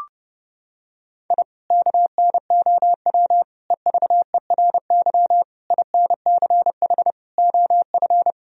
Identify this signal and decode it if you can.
Morse code, heard as a keyed tone, about 30 wpm, 700 Hz. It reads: I KNOW EVERY INCH OF
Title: Sentences from Top 1000 Words - Review Part 3 - 30wpm